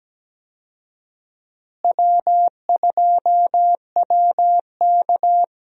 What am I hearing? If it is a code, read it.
Morse code, heard as a keyed tone, about 17 wpm, 700 Hz. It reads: W2WK
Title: USA Call Signs - 4 Characters - Repeated 3x - 17wpm